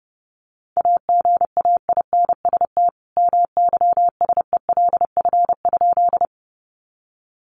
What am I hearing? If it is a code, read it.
Morse code, heard as a keyed tone, about 30 wpm, 700 Hz. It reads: AGAINST MYSELF?
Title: Sentences from Top 1300 Words 30wpm